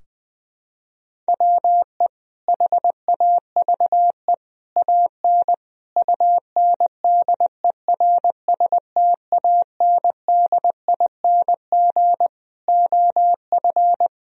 Morse code, heard as a keyed tone, about 20 wpm, 700 Hz. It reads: WE HAVE AN UNDERSTANDING OF
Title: Sentences from Top 1000 Words - Review Part 3 - 20wpm